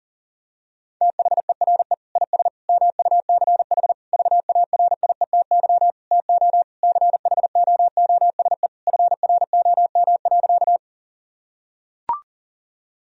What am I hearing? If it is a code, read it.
Morse code, heard as a keyed tone, about 37 wpm, 700 Hz. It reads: THERE IS MUCH VARIETY TO CHOOSE FROM.
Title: Sentences from Top 1000 Words - Review Part 4 - 40wpm